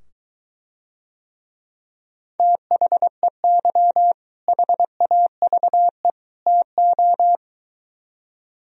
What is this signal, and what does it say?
Morse code, heard as a keyed tone, about 23 wpm, 700 Hz. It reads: THEY HAVE TO